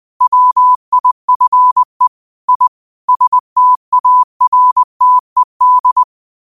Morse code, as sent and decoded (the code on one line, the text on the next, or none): .-- .. ..-. / .. / ... - .- .-. - . -..
WIFE I STARTED